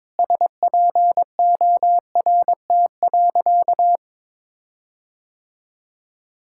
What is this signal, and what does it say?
Morse code, heard as a keyed tone, about 22 wpm, 700 Hz. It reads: SPORT.
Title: Sentences from Top 1600 Words 22wpm